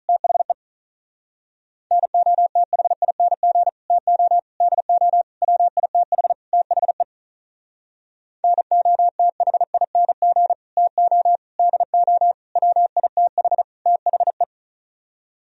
Morse code, heard as a keyed tone, about 41 wpm, 700 Hz. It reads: THE NOTHING TO DO WITH THE NOTHING TO DO WITH THE
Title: Speed-Racing - Sets of 5 Words 35wpm